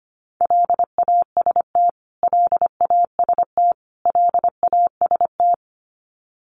Morse code, heard as a keyed tone, about 25 wpm, 700 Hz. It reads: LAST LAST LAST
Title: Top 500 Words Repeated 25wpm